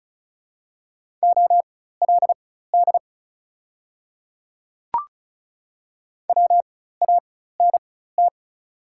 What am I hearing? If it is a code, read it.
Morse code, heard as a keyed tone, about 36 wpm, spaced as for 16 wpm, 690 Hz. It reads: OLD WANT
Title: 4x Character Spacing - Top 500 Words - 35wpm